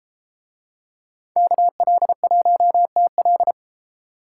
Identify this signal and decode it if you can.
Morse code, heard as a keyed tone, about 33 wpm, 700 Hz. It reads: KL1TL